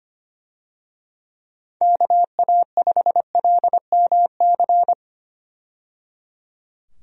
Morse code, heard as a keyed tone, about 25 wpm, 700 Hz. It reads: KA5LMC
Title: USA Call Signs - 6 Characters - 25wpm